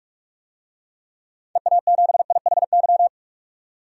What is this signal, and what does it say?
Morse code, heard as a keyed tone, about 41 wpm, 700 Hz. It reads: EA7IHY
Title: DXCC-100 Call Signs - 6 Characters - 45wpm